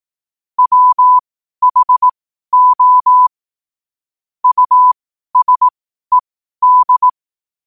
Morse code, heard as a keyed tone, about 18 wpm, spaced as for 12 wpm, 1000 Hz. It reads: WHO USED
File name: qotdfiles / 12wpm